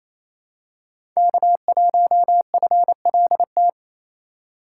Morse code, heard as a keyed tone, about 28 wpm, 700 Hz. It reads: K1FLT
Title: USA Call Signs - 5 Characters - 6x Character Spacing - 28wpm